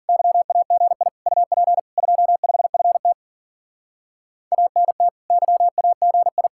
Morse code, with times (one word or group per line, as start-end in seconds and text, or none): 0.1-1.1 YAGI
1.3-1.8 UP
2.0-3.1 25FT
4.5-5.1 ANT
5.3-6.5 YAGI